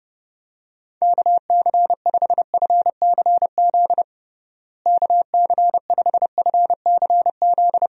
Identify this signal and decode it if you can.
Morse code, heard as a keyed tone, about 30 wpm, 700 Hz. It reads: KC5FCZ KC5FCZ
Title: USA Call Signs - 6 Characters - Repeated 2x - 30wpm